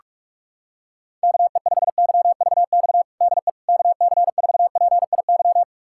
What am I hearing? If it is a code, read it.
Morse code, heard as a keyed tone, about 46 wpm, 700 Hz. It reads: KE5YVX DE KC4PIY
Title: USA Callsign DE Callsign - 6 Characters - 45wpm